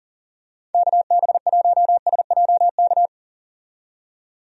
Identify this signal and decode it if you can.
Morse code, heard as a keyed tone, about 40 wpm, 700 Hz. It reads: KB1SJK
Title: USA Call Signs - 6 Characters - 40wpm